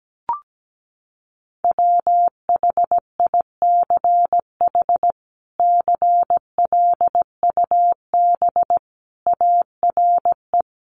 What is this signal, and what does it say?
Morse code, heard as a keyed tone, about 17 wpm, 700 Hz. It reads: WHICH CLUB ARE
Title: Sentences from Top 1200 Words 17wpm